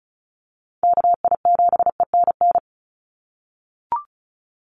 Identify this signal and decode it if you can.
Morse code, heard as a keyed tone, about 34 wpm, 700 Hz. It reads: KI7ENN
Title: USA Call Signs 35wpm